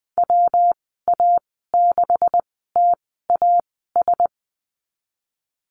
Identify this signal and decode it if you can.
Morse code, heard as a keyed tone, about 20 wpm, spaced as for 15 wpm, 700 Hz. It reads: WA6TAS
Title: 140 Lesson 23g - Letter f - Call Signs 20wpm